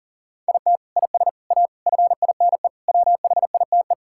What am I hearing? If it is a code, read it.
Morse code, heard as a keyed tone, about 40 wpm, 700 Hz. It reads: IT IS A FINE WHITE
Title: Sentences from Top 1600 Words 40wpm